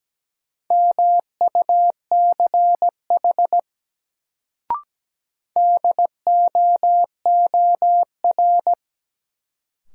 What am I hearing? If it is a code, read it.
Morse code, heard as a keyed tone, about 17 wpm, 700 Hz. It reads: MUCH DOOR
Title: Sets of 2 Words - New Word Drill - 17wpm